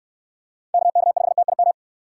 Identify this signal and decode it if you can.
Morse code, heard as a keyed tone, about 65 wpm, 700 Hz. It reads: KK5TIM